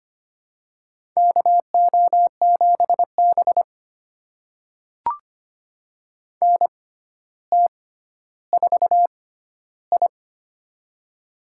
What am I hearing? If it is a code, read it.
Morse code, heard as a keyed tone, about 25 wpm, spaced as for 13 wpm, 700 Hz. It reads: KO7B NT4I
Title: USA Call Signs - 4 Characters - 6x Character Spacing - 25wpm